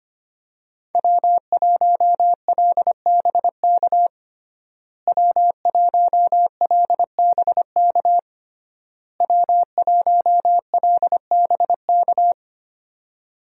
Morse code, heard as a keyed tone, about 25 wpm, 700 Hz. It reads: W1LBK W1LBK W1LBK
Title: USA Call Signs - 5 Characters - Repeated 3x - 25wpm